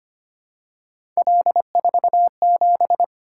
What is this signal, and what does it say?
Morse code, heard as a keyed tone, about 25 wpm, 700 Hz. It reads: L47
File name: Three Letter-Number Combinations - 25wpm